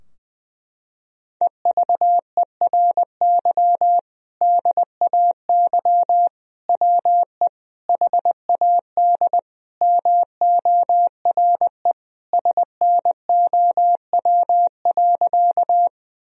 Morse code, heard as a keyed tone, about 20 wpm, 700 Hz. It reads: EVERY DAY WE HAD MORE SNOW.
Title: Sentences from Top 1000 Words - Review Part 3 - 20wpm